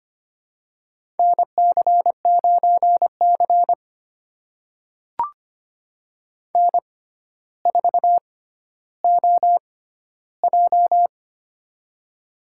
Morse code, heard as a keyed tone, about 25 wpm, spaced as for 13 wpm, 700 Hz. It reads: NC9C N4OJ
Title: USA Call Signs - 4 Characters - 6x Character Spacing - 25wpm